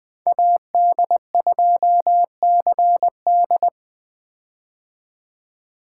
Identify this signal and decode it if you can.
Morse code, heard as a keyed tone, about 20 wpm, 700 Hz. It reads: AD2CD
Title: 2023 POTA Hunter Call Signs - 20wpm